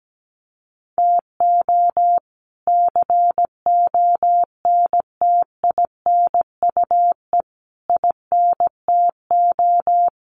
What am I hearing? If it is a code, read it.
Morse code, heard as a keyed tone, about 17 wpm, 700 Hz. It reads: TO CONTINUE INTO
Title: Sentences from Top 1500 Words 17wpm